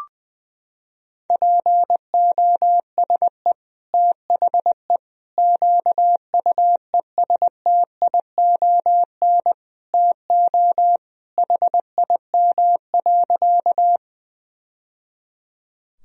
Morse code, heard as a keyed tone, about 20 wpm, 700 Hz. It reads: POSE THE QUESTION TO HIM.
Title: Sentences from Top 1000 Words - Review Part 2 - 20wpm